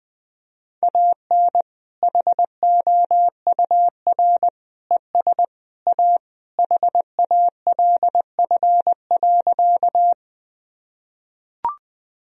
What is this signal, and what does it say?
Morse code, heard as a keyed tone, about 20 wpm, 700 Hz. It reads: AN HOUR ES A HALF.
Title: Sentences from Top 400 Words and Abbreviations 20wpm